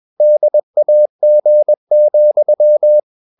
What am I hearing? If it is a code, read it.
Morse code, heard as a keyed tone, about 21 wpm, 600 Hz. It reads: DAG,
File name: Ferie QRS 21wpm